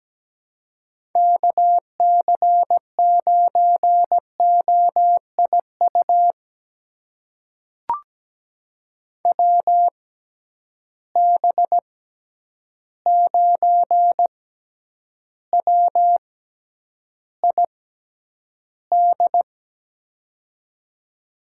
Morse code, heard as a keyed tone, about 17 wpm, spaced as for 9 wpm, 700 Hz. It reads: KC9OIU WB9WID
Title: USA Call Signs - 6 Characters - 6x Character Spacing - 17wpm